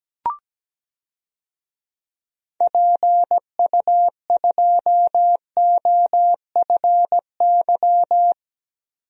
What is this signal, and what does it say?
Morse code, heard as a keyed tone, about 17 wpm, 700 Hz. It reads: PU2OFY
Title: DXCC-100 Call Signs - 6 Characters - 17wpm